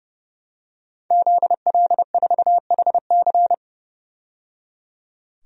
Morse code, heard as a keyed tone, about 31 wpm, 700 Hz. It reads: ZL4HC
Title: DXCC-100 Call Signs - 5 Characters - 30wpm